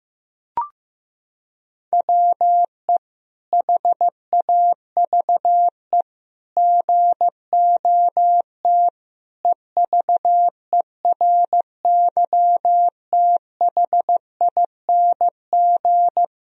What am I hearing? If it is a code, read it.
Morse code, heard as a keyed tone, about 16 wpm, 700 Hz. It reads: WE HAVE GOT EVERYTHING
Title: Sentences from Top 1000 Words - Review Part 3 - 15wpm